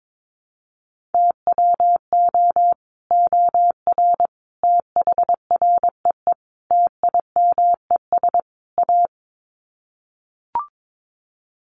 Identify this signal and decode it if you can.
Morse code, heard as a keyed tone, about 22 wpm, 700 Hz. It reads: TWO OR THREE TIMES A
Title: Sets of 5 Words Encore2 22wpm